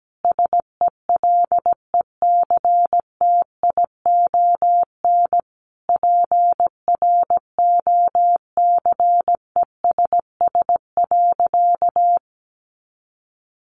Morse code, heard as a keyed tone, about 17 wpm, 700 Hz. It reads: SELECTION PROCESS.
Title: Sentences from Top 1400 Words 17wpm